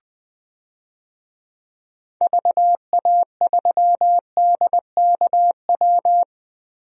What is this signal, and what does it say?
Morse code, heard as a keyed tone, about 20 wpm, 700 Hz. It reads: VA3DKW
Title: DXCC-100 Call Signs - 6 Characters - Repeated 3x - 20wpm